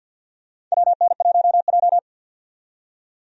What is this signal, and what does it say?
Morse code, heard as a keyed tone, about 50 wpm, 700 Hz. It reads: WN1J
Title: USA Call Signs - 4 Characters - 50wpm